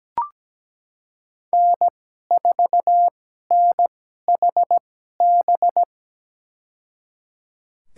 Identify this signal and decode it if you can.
Morse code, heard as a keyed tone, about 17 wpm, spaced as for 12 wpm, 700 Hz. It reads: N4NHB
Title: USA Call Signs - 5 Characters - 2x Character Spacing - 17wpm